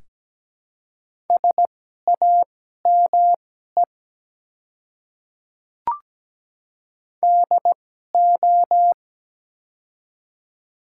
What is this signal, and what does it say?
Morse code, heard as a keyed tone, about 17 wpm, spaced as for 12 wpm, 700 Hz. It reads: SAME DO
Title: 2x Character Spacing - Top 200 Words - 17wpm